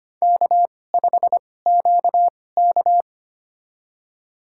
Morse code, heard as a keyed tone, about 25 wpm, spaced as for 18 wpm, 700 Hz. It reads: K5QK
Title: USA Call Signs - 2x Character Spacing - 25wpm